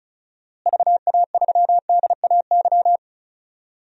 Morse code, heard as a keyed tone, about 36 wpm, 700 Hz. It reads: VA3DAY
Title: DXCC-100 Call Signs - 6 Characters - Repeated 3x - 35wpm